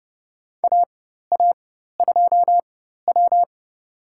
Morse code, heard as a keyed tone, about 30 wpm, spaced as for 14 wpm, 700 Hz. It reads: AA2W
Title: USA Call Signs - 4 Characters - 4x Character Spacing - 30wpm